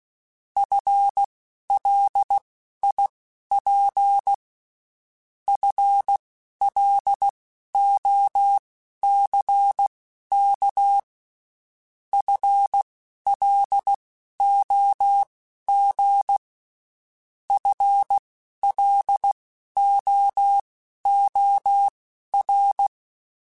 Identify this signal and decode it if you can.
Morse code, heard as a keyed tone, about 16 wpm, spaced as for 11 wpm, 800 Hz. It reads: FLIP FLOCK FLOG FLOOR
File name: Spaced / FL Runs sp mp3s / FL Runs sp @15